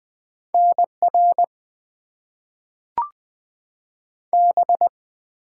Morse code, heard as a keyed tone, about 20 wpm, 700 Hz. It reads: NR B
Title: Top 90 QSO Elements - 6x Character Spacing - 20wpm